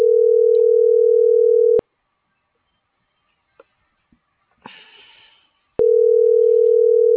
Ringback tone (North America), 440 + 480 Hz.